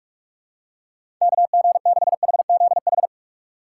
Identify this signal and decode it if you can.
Morse code, heard as a keyed tone, about 44 wpm, 700 Hz. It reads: KG6HZH
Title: USA Call Signs - 6 Characters - 4x Character Spacing - 45wpm